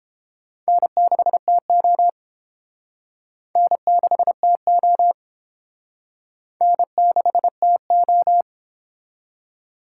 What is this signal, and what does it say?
Morse code, heard as a keyed tone, about 29 wpm, 700 Hz. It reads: N6TO N6TO N6TO